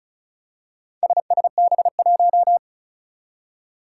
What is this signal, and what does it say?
Morse code, heard as a keyed tone, about 36 wpm, 700 Hz. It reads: SSB1
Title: Four Challenging Letter-Number Combinations 35wpm